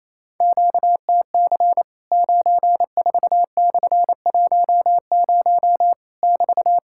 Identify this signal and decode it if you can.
Morse code, heard as a keyed tone, about 28 wpm, 700 Hz. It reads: QTC 94/10 <BT>
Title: Worked All Europe DX Contest - QTC Exchange - 28wpm